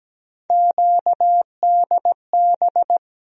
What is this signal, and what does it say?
Morse code, heard as a keyed tone, about 17 wpm, 700 Hz. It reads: QDB
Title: USA Call Sign Postfixes - 3 Letters 17wpm